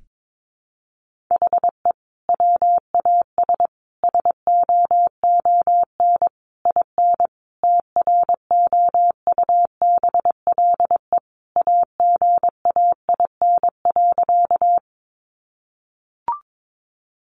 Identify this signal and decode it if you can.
Morse code, heard as a keyed tone, about 22 wpm, 700 Hz. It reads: HE WAS SOON IN TROUBLE AGAIN.
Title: Sentences from Top 1200 Words 22wpm